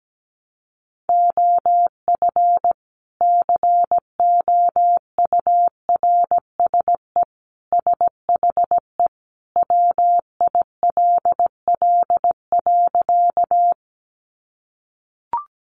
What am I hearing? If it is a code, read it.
Morse code, heard as a keyed tone, about 17 wpm, 700 Hz. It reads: OF COURSE SHE WILL.
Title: Sentences from Top 500 Words 17wpm